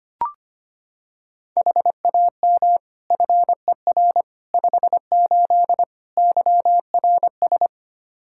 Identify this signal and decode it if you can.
Morse code, heard as a keyed tone, about 25 wpm, 700 Hz. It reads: HAM FER 58 YRS
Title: K1USN SST QSO Practice 25wpm